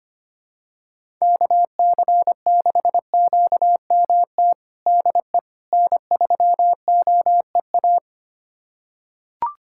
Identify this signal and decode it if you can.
Morse code, heard as a keyed tone, about 25 wpm, 700 Hz. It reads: KC6QMT DE N3OEA
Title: USA Callsign DE Callsign 25wpm